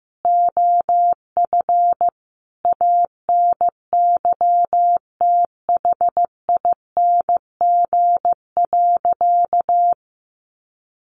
Morse code, heard as a keyed tone, about 15 wpm, 700 Hz. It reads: OF ANYTHING.
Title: Sentences from Top 1200 Words 15wpm